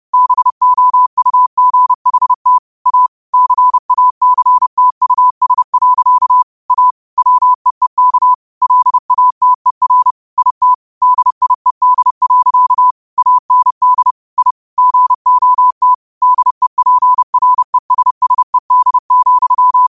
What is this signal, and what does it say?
Morse code, heard as a keyed tone, about 30 wpm, 1000 Hz. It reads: BOUGHT A CACTUS. A WEEK LATER IT DIED. AND I GOT DEPRESSED,
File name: qotdfiles / 30wpm